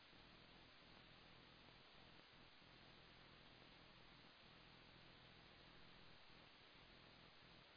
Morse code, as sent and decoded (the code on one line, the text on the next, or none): none